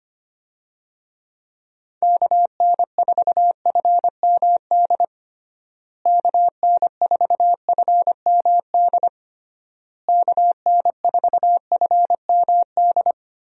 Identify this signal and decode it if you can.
Morse code, heard as a keyed tone, about 25 wpm, 700 Hz. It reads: KN4FMD KN4FMD KN4FMD
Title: USA Call Signs - 6 Characters - Repeated 3x - 25wpm